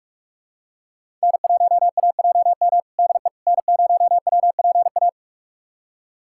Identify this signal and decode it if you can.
Morse code, heard as a keyed tone, about 46 wpm, 700 Hz. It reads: N1AJM DE N0WPA
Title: USA Callsign DE Callsign 45wpm